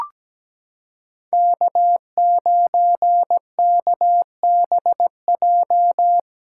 Morse code, heard as a keyed tone, about 17 wpm, 700 Hz. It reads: K9KBJ
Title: USA Callsign DE Callsign - 5 Characters - 17wpm